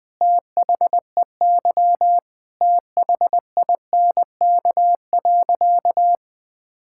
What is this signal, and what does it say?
Morse code, heard as a keyed tone, about 20 wpm, 700 Hz. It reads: THEY THINK.
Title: Sentences with 5-Word Phrases - 20wpm